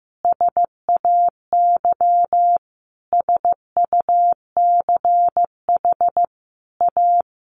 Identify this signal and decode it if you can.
Morse code, heard as a keyed tone, about 15 wpm, 700 Hz. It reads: SAY SUCH A
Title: Sentences from Top 200 Words 15wpm